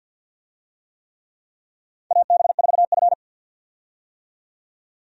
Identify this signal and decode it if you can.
Morse code, heard as a keyed tone, about 47 wpm, 700 Hz. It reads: AB4F
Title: USA Call Signs - 4 Characters - 50wpm